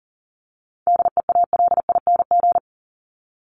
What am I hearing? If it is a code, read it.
Morse code, heard as a keyed tone, about 40 wpm, 700 Hz. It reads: DEALING
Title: ING Suffix - Top 100 Words with Base Words 40wpm